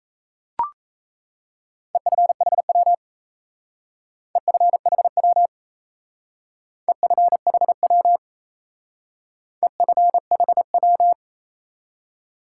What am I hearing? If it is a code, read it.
Morse code, heard as a keyed tone, about 37 wpm, 700 Hz. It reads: EFHW EFHW EFHW EFHW